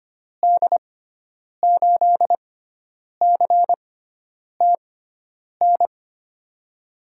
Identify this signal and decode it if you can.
Morse code, heard as a keyed tone, about 25 wpm, spaced as for 9 wpm, 700 Hz. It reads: D8CTN
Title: USA Call Signs - 6 Characters - 6x Character Spacing - 25wpm